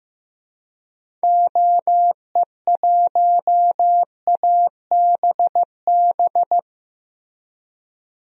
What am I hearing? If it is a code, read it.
Morse code, heard as a keyed tone, about 15 wpm, 700 Hz. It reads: OE1ABB
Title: DXCC-100 Call Signs - 6 Characters - 15wpm